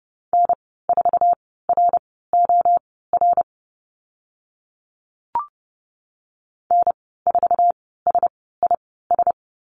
Morse code, heard as a keyed tone, about 30 wpm, spaced as for 17 wpm, 700 Hz. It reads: N4ROR N4SIS
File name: Lesson 15g - Number 5 - Call Signs 30wpm